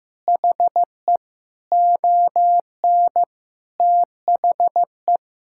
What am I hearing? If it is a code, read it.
Morse code, heard as a keyed tone, about 16 wpm, 700 Hz. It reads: HE ON THE
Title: Sentences from Top 1000 Words - Review Part 6 - 15wpm